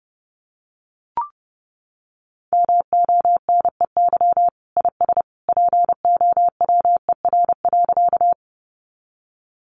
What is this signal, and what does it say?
Morse code, heard as a keyed tone, about 30 wpm, 700 Hz. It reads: MONEY IS POWER.
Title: Sentences from Top 500 Words 30wpm